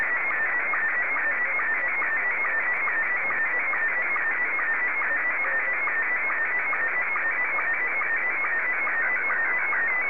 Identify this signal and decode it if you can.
SSTV (slow-scan television), Scottie 1.